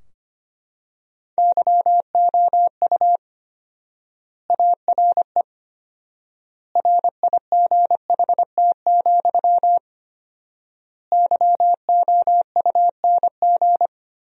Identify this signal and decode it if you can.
Morse code, heard as a keyed tone, about 25 wpm, 700 Hz. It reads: YOU ARE RIGHT, YOUNG
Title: 4x Word-Spacing - Sentences from Top 400 Words - 25wpm